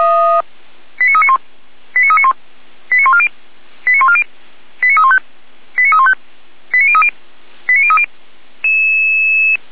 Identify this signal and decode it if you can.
Five-tone selective call (ZVEI), heard as quick runs of five tones, tones of about 70 ms. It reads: 89391 89391 89140 89140 89315 89315 89030 89030